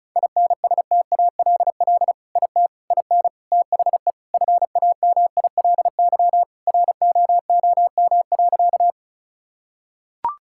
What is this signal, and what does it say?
Morse code, heard as a keyed tone, about 36 wpm, 700 Hz. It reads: INSTALL IT IN THE FAMILY ROOM.
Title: Sentences from Top 1600 Words 35wpm